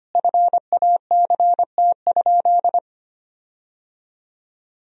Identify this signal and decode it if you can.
Morse code, heard as a keyed tone, about 25 wpm, 700 Hz. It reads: FACT?
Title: 4x Word-Spacing - Sentences from Top 500 Words - 25wpm